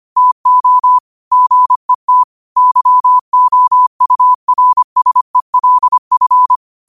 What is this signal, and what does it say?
Morse code, heard as a keyed tone, about 25 wpm, 1000 Hz. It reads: TO GET YOURSELF
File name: qotdfiles / 25wpm